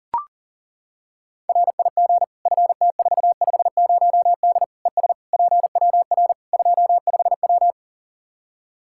Morse code, heard as a keyed tone, about 40 wpm, 700 Hz. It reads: RIG FT450D ES PWR 25W
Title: K1USN SST QSO Practice 40wpm